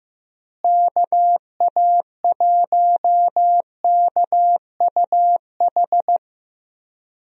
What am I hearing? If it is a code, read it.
Morse code, heard as a keyed tone, about 15 wpm, 700 Hz. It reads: KA1KUH
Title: USA Call Signs - 6 Characters - Repeated 3x - 15wpm